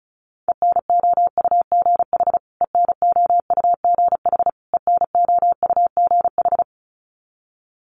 Morse code, heard as a keyed tone, about 34 wpm, 700 Hz. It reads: ENOUGH ENOUGH ENOUGH
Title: Top 500 Words Repeated 35wpm